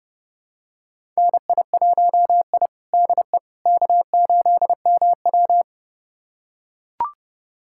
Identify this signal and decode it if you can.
Morse code, heard as a keyed tone, about 32 wpm, 700 Hz. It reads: NI1I DE K8MW
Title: USA Callsign DE Callsign - 4 Characters - 30wpm